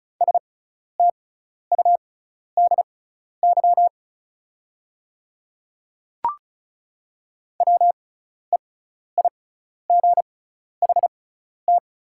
Morse code, heard as a keyed tone, about 36 wpm, spaced as for 12 wpm, 690 Hz. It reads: STUDY WEIGHT
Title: 6x Character Spacing - Top 500 Words - 35wpm